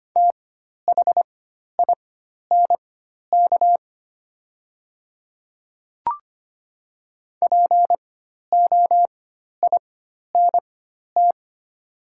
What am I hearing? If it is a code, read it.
Morse code, heard as a keyed tone, about 25 wpm, spaced as for 12 wpm, 690 Hz. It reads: THINK POINT
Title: 4x Character Spacing - Top 200 Words - 25wpm